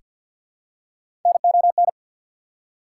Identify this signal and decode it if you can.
Morse code, heard as a keyed tone, about 48 wpm, 690 Hz. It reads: NON
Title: Prefixes and Suffixes 50wpm